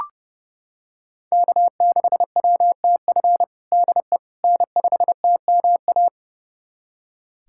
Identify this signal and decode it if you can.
Morse code, heard as a keyed tone, about 30 wpm, 700 Hz. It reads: K6WTF DE N5TMA
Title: USA Callsign DE Callsign - 5 Characters - 30wpm